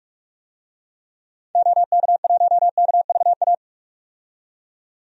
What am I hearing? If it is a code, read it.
Morse code, heard as a keyed tone, about 46 wpm, 700 Hz. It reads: OK1KVA
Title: DXCC-100 Call Signs - 6 Characters - 45wpm